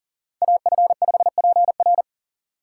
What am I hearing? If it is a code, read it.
Morse code, heard as a keyed tone, about 40 wpm, 700 Hz. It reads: AF5PR
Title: USA Call Signs - 5 Characters - 2x Character Spacing - 40wpm